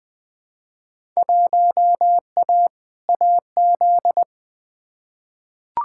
Morse code, heard as a keyed tone, about 20 wpm, 700 Hz. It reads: 1A AZ
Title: Field Day Exchanges - 20wpm